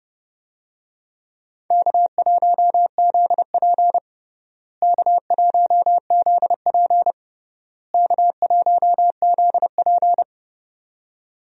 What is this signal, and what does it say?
Morse code, heard as a keyed tone, about 30 wpm, 700 Hz. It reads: K1ZP K1ZP K1ZP
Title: USA Call Signs - 4 Characters - Repeated 3x - 30wpm